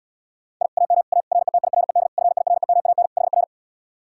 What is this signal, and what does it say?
Morse code, heard as a keyed tone, about 79 wpm, 700 Hz. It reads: I AM A RETIRED FIREMAN HR